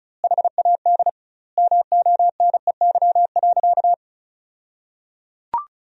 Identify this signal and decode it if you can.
Morse code, heard as a keyed tone, about 34 wpm, 700 Hz. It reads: HAD MONEY.
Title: 2x Word-Spacing - Sentences from Top 500 Words - 35wpm